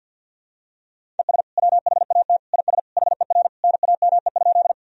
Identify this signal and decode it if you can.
Morse code, heard as a keyed tone, about 52 wpm, 700 Hz. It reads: ES WHAT IS HER NAME?
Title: Sentences from Top 200 Words and Abbreviations 50wpm